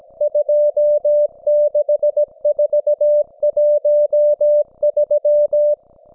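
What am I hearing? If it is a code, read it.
Morse code, heard as a keyed tone, about 17 wpm, 590 Hz. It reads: 26413
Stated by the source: Russian military Morse station, M32, RDL, on 5268 kHz.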